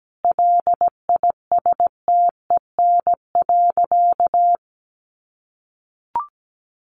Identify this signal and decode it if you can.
Morse code, heard as a keyed tone, about 17 wpm, 700 Hz. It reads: LISTEN.
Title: Sentences from Top 500 Words 17wpm